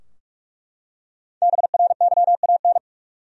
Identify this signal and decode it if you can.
Morse code, heard as a keyed tone, about 47 wpm, 700 Hz. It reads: BRYAN
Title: USA Common Names - Max 5 Letters - 45wpm